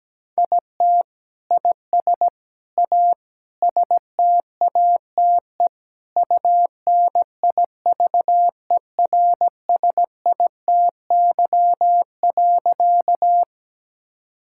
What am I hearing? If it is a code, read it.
Morse code, heard as a keyed tone, about 17 wpm, 700 Hz. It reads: IT IS A STATE UNIVERSITY.
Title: Sentences from Top 1000 Words - Review Part 5 - 17wpm